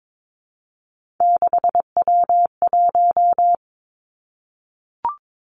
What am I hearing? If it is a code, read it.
Morse code, heard as a keyed tone, about 22 wpm, 700 Hz. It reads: 6W1
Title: Three Challenging Letter-Number Combinations 22wpm